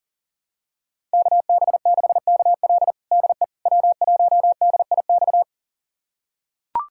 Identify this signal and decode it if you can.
Morse code, heard as a keyed tone, about 37 wpm, 700 Hz. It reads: KB6KL DE W1DIX
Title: USA Callsign DE Callsign 40wpm